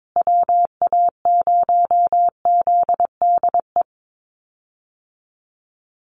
Morse code, heard as a keyed tone, about 22 wpm, 700 Hz. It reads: WA0ZDE
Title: USA Call Signs Repeated 3x 22wpm